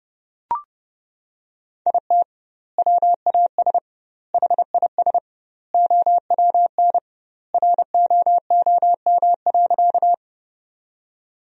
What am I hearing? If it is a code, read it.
Morse code, heard as a keyed tone, about 32 wpm, 700 Hz. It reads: IT WAS HIS OWN ROOM.
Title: 2x Word-Spacing - Sentences from Top 300 Words - 30wpm